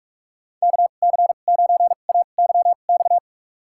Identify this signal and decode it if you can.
Morse code, heard as a keyed tone, about 41 wpm, spaced as for 31 wpm, 700 Hz. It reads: KC9AYX
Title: USA Call Signs - 6 Characters - 2x Character Spacing - 45wpm